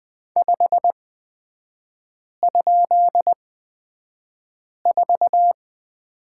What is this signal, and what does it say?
Morse code, heard as a keyed tone, about 20 wpm, spaced as for 5 wpm, 700 Hz. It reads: 5?4
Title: 243 Lesson 39e - Letter z - Three Characters 20wpm